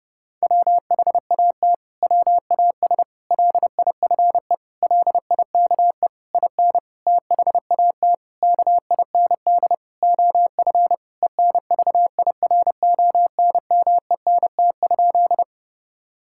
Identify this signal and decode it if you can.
Morse code, heard as a keyed tone, about 32 wpm, 700 Hz. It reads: WHAT WAS LIFE LIKE IN THAT KIND OF ENVIRONMENT?
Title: Sentences from Top 1000 Words - Review Part 5 - 30wpm